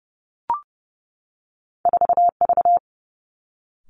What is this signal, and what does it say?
Morse code, heard as a keyed tone, about 30 wpm, 700 Hz. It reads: 4V
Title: Two Challenging Letter-Number Combinations 30wpm